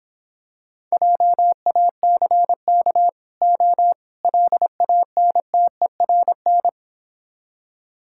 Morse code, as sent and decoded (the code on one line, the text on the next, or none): .--- .- -.-. -.- / --- / .-.. .- -. - . .-. -.
JACK O LANTERN